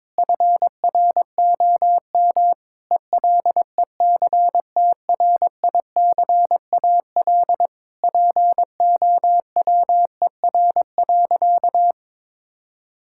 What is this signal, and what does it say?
Morse code, heard as a keyed tone, about 22 wpm, 700 Hz. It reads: FROM ELECTRICAL POWER.
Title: Sentences from Top 1000 Words - Review Part 6 - 22wpm